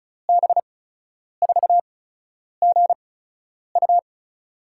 Morse code, heard as a keyed tone, about 36 wpm, spaced as for 10 wpm, 700 Hz. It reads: B4GU
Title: USA Call Signs - 5 Characters - 8x Character Spacing - 35wpm